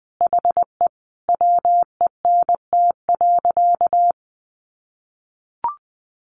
Morse code, heard as a keyed tone, about 20 wpm, 700 Hz. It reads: HE WENT.
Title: Sentences from Top 1400 Words 20wpm